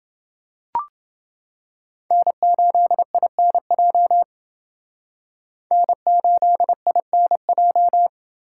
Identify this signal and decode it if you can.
Morse code, heard as a keyed tone, about 29 wpm, 700 Hz. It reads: N8INJ N8INJ